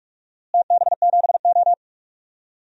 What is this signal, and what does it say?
Morse code, heard as a keyed tone, about 43 wpm, 700 Hz. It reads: TB7O
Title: DXCC-100 Call Signs - 4 Characters - Repeated 3x - 45wpm